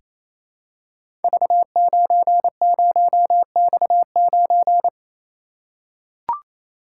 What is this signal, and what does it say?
Morse code, heard as a keyed tone, about 29 wpm, 700 Hz. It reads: V90X9
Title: Code Groups - 5 Letter:Numbers - 28wpm